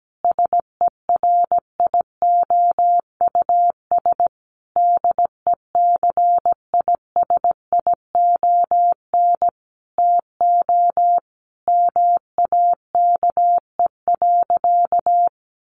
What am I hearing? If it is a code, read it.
Morse code, heard as a keyed tone, about 17 wpm, 700 Hz. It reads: SERIOUS DECISION TO MAKE.
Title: Sentences from Top 1100 Words 17wpm